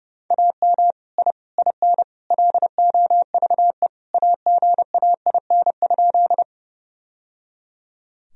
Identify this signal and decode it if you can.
Morse code, heard as a keyed tone, about 30 wpm, 700 Hz. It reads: AM I IN LOVE AGAIN?
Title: Sentences from Top 1000 Words - Review Part 2 - 30wpm